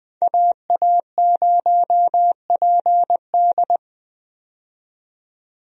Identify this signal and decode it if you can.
Morse code, heard as a keyed tone, about 21 wpm, 700 Hz. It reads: AA0PD
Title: USA Call Signs - 5 Characters - 20wpm